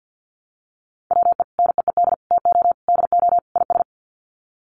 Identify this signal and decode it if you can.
Morse code, heard as a keyed tone, about 50 wpm, 700 Hz. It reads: WE NEED TO DO IS
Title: Sets of 5 Words Encore2 50wpm